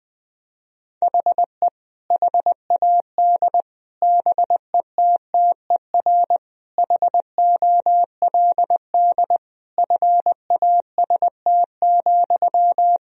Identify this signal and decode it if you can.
Morse code, heard as a keyed tone, about 20 wpm, 700 Hz. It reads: HE HAD BETTER HOLD FAST,
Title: Sentences from Top 1000 Words - Review Part 2 - 20wpm